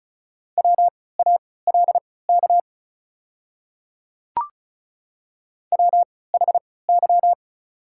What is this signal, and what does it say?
Morse code, heard as a keyed tone, about 36 wpm, spaced as for 20 wpm, 700 Hz. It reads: WALK WHY
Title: 3x Character Spacing - Top 300 Words - 35wpm